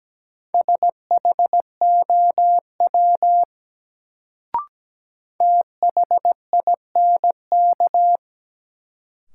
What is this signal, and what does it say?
Morse code, heard as a keyed tone, about 17 wpm, 700 Hz. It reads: SHOW THINK
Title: Sentences from Top 200 Words - New Word Drill - 17wpm